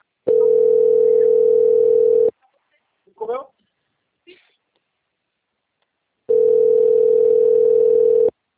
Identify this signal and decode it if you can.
Ringback tone (North America), 440 + 480 Hz, with speech.
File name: out- - - - - .3564